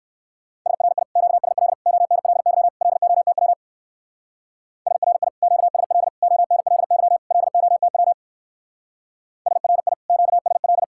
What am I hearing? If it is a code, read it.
Morse code, heard as a keyed tone, about 58 wpm, 700 Hz. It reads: SRI QSL ONLY LOTW SRI QSL ONLY LOTW SRI QSL